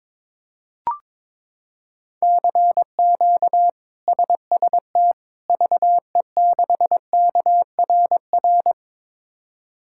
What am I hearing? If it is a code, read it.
Morse code, heard as a keyed tone, about 22 wpm, 700 Hz. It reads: CQ SST VE6KRR
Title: K1USN SST QSO Practice - 22wpm